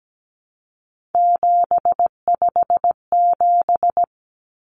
Morse code, heard as a keyed tone, about 17 wpm, 700 Hz. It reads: 757
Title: Common Cut Numbers - 3-Digit 17wpm